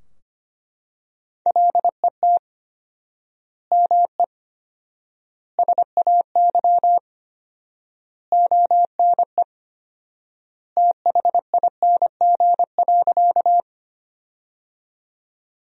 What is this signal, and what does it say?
Morse code, heard as a keyed tone, about 25 wpm, 700 Hz. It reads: LET ME SAY ONE THING.
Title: 4x Word-Spacing - Sentences from Top 300 Words - 25wpm